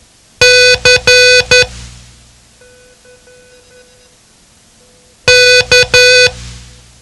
Morse code, heard as a keyed tone, about 11 wpm, 500 Hz. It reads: C K
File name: ham radio / CodeQuick / Code Quick 07